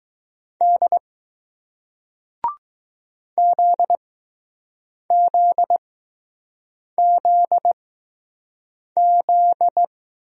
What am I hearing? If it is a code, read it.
Morse code, heard as a keyed tone, about 20 wpm, spaced as for 6 wpm, 700 Hz. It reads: D ZZZZ